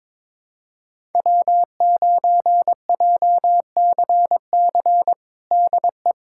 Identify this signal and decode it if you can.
Morse code, heard as a keyed tone, about 22 wpm, 700 Hz. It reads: W9JCC DE
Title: USA Callsign DE Callsign - 5 Characters - 22wpm